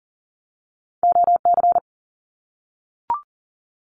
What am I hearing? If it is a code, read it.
Morse code, heard as a keyed tone, about 40 wpm, 700 Hz. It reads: OC
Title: Abbreviations Repeated 3x - 40wpm